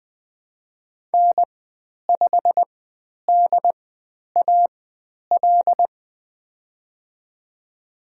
Morse code, heard as a keyed tone, about 20 wpm, spaced as for 10 wpm, 700 Hz. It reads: N5DAL
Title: 095 Lesson 16g - Letter c - Call Signs 20wpm